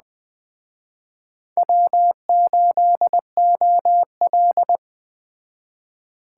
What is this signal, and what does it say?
Morse code, heard as a keyed tone, about 21 wpm, 700 Hz. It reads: W8OL